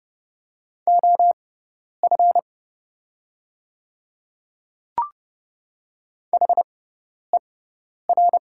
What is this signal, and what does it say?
Morse code, heard as a keyed tone, about 30 wpm, spaced as for 10 wpm, 690 Hz. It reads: OF HER
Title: 6x Character Spacing - Top 100 Words - 30wpm